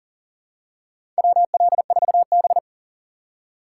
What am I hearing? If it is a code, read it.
Morse code, heard as a keyed tone, about 40 wpm, 700 Hz. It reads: WL4B
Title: USA Call Signs - 4 Characters - 2x Character Spacing - 40wpm